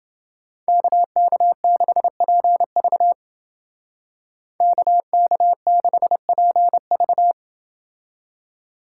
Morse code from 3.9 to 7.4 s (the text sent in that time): KK6PV